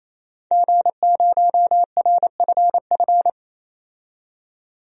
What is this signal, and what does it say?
Morse code, heard as a keyed tone, about 28 wpm, 700 Hz. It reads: G0RFF
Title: DXCC-100 Call Signs - 5 Characters - 28wpm